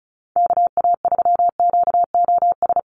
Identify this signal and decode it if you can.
Morse code, heard as a keyed tone, about 34 wpm, 700 Hz. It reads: KA3QOS
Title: USA Call Signs 35wpm